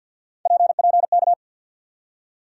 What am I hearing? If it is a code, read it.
Morse code, heard as a keyed tone, about 49 wpm, 690 Hz. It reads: PPK
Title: USA Call Sign Postfixes - 3 Letters 50wpm